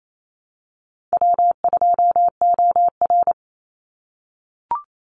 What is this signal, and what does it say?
Morse code, heard as a keyed tone, about 28 wpm, 700 Hz. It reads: W2OR
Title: Easy Call Signs - 28wpm